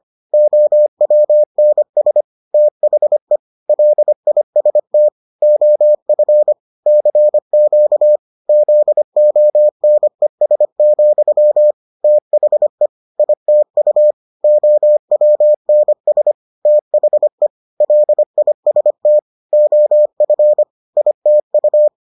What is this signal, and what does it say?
Morse code, heard as a keyed tone, about 25 wpm, 600 Hz. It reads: OWNS THE LIST OF CQ ZONES, THE ITU OWNS THE LIST OF ITU